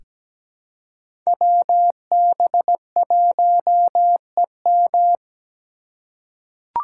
Morse code, heard as a keyed tone, about 17 wpm, 700 Hz. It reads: WB1EM
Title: Code Groups - 5 Letter:Numbers - 17wpm